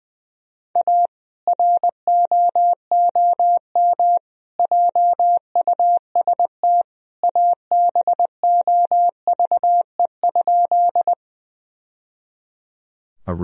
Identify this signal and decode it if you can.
Morse code, heard as a keyed tone, about 20 wpm, 700 Hz. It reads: A ROOM JUST ABOVE?
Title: Sentences from Top 1000 Words - Review Part 2 - 20wpm